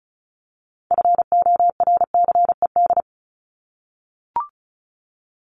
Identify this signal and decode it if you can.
Morse code, heard as a keyed tone, about 34 wpm, 700 Hz. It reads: FORCED
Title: ED Suffix - Top 100 Words with Base Words 35wpm